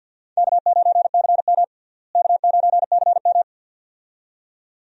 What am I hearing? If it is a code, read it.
Morse code, heard as a keyed tone, about 50 wpm, 700 Hz. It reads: K9CM K9CM
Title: USA Call Signs - 4 Characters - Repeated 2x - 50wpm